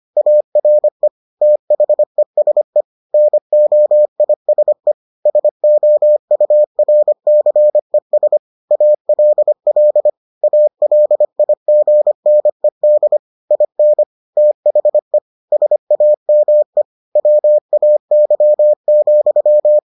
Morse code, heard as a keyed tone, about 25 wpm, 600 Hz. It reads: ARE THESE NOISE SOURCES ALL ALIGNED IN THE SAME WAY,